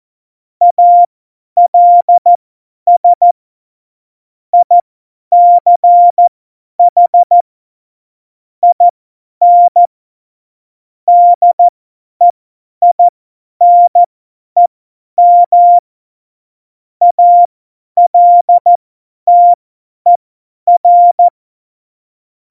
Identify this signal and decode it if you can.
Morse code, heard as a keyed tone, about 14 wpm, spaced as for 10 wpm, 700 Hz. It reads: ALS ICH IN DEINEM ALTER
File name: Morsekurs / Witze / Witz 10.Juli 2018, 50 F70